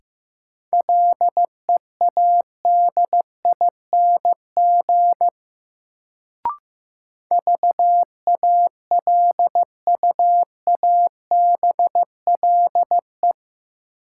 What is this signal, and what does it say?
Morse code, heard as a keyed tone, about 15 wpm, 700 Hz. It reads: LEADING VALUABLE
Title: Sentences from Top 1100 Words - New Word Drill - 15wpm